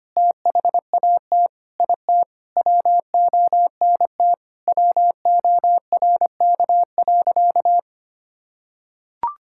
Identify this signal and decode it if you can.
Morse code, heard as a keyed tone, about 25 wpm, 700 Hz. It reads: THAT IT WONT WORK.
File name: Sentences with 3-Word Phrases - 25wpm